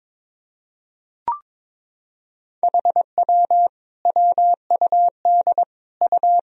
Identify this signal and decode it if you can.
Morse code, heard as a keyed tone, about 22 wpm, 700 Hz. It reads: HW WUD U
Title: Sentences from Top 100 Words and Abbreviations 22wpm